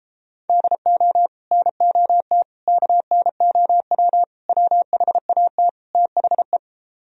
Morse code, heard as a keyed tone, about 31 wpm, 700 Hz. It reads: DO NOT KNOW WHAT THE